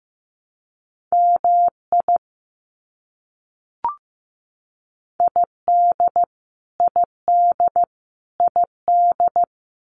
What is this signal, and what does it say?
Morse code, heard as a keyed tone, about 15 wpm, 700 Hz. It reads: MI ID ID ID
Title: US States Repeated 3x 15wpm